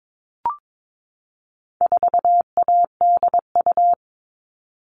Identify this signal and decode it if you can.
Morse code, heard as a keyed tone, about 22 wpm, 700 Hz. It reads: 4ADU
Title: All Cut Numbers - 4-Digit 22wpm